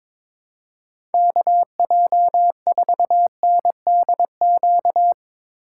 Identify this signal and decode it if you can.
Morse code, heard as a keyed tone, about 23 wpm, 700 Hz. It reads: KJ4NDQ
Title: Speed Racing - USA Call Signs 22wpm